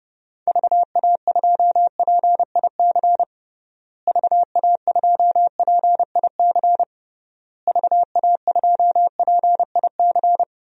Morse code, heard as a keyed tone, about 31 wpm, 700 Hz. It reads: VA2PIC VA2PIC VA2PIC
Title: DXCC-100 Call Signs - 6 Characters - Repeated 3x - 30wpm